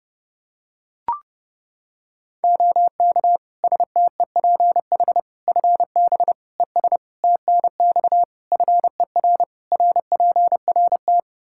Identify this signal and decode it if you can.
Morse code, heard as a keyed tone, about 32 wpm, 700 Hz. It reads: OK STEPH FB ES TNX FER RPRT
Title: Practice QSO - Exchange 2 - Info - 30wpm